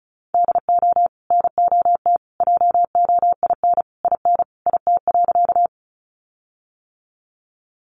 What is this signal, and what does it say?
Morse code, heard as a keyed tone, about 34 wpm, 700 Hz. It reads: DO NOT JOIN IN IT.
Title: Sentences from Top 1400 Words 35wpm